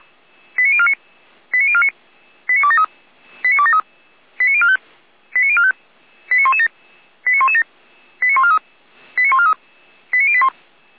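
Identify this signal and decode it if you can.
Five-tone selective call (ZVEI), heard as quick runs of five tones, tones of about 70 ms. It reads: 89049 89049 89273 89045 89045 89117 89117 89143 89143 89081
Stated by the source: ZVEI five-tone selective call received on SDR.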